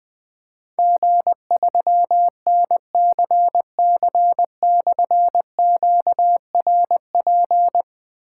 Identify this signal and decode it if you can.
Morse code, heard as a keyed tone, about 20 wpm, 700 Hz. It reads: G3NCC/QRP
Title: Difficult Call Signs 20wpm